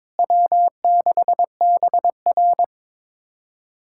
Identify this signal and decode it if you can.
Morse code, heard as a keyed tone, about 22 wpm, 700 Hz. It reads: W6BR
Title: USA Callsign DE Callsign - 4 Characters - 22wpm